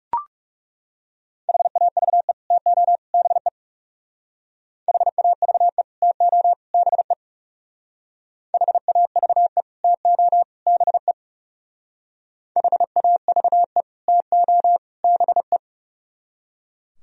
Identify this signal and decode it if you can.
Morse code, heard as a keyed tone, about 37 wpm, 700 Hz. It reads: HAVE TO BE HAVE TO BE HAVE TO BE HAVE TO BE
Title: Speed-Racing - Sets of 3 Words 30wpm